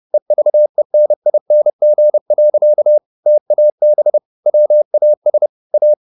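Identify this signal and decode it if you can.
Morse code, heard as a keyed tone, about 30 wpm, 600 Hz. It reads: EVENING. TAB WAS A
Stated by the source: Morse code at 30 wpm, the novel Never by Ken Follett in English, from CW med Gnister.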